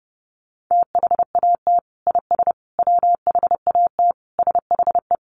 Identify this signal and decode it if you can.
Morse code, heard as a keyed tone, about 30 wpm, 700 Hz. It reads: THAT IS WHAT SHE
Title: Sentences from Top 100 Words 30wpm